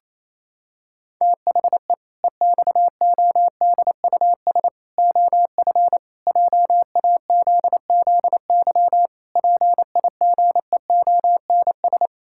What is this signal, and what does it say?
Morse code, heard as a keyed tone, about 28 wpm, 700 Hz. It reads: THE EXODUS OF JAZZY PIGEONS
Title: Pangram Challenge v2 - 28wpm